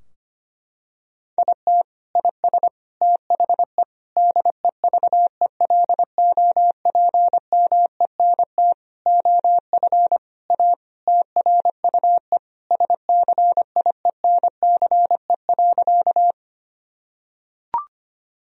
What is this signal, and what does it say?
Morse code, heard as a keyed tone, about 25 wpm, 700 Hz. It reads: IT IS THE DEVELOPMENT OF A TRUE SCIENCE.
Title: Sentences from Top 1000 Words - Review Part 4 - 25wpm